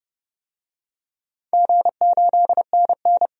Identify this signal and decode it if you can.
Morse code, heard as a keyed tone, about 32 wpm, 700 Hz. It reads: G8NN
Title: DXCC-100 Call Signs - 4 Characters - 30wpm